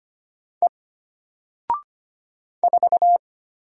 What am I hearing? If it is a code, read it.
Morse code, heard as a keyed tone, about 25 wpm, 700 Hz. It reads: E 4
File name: Single-Letter-Number-25wpm